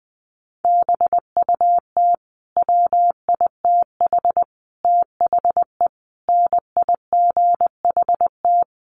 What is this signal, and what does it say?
Morse code, heard as a keyed tone, about 20 wpm, 700 Hz. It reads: BUT WITH THE NIGHT
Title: Sentences from Top 1500 Words 20wpm